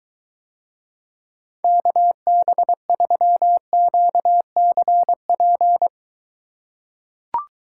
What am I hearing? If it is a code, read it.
Morse code, heard as a keyed tone, about 22 wpm, 700 Hz. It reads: KB3QCP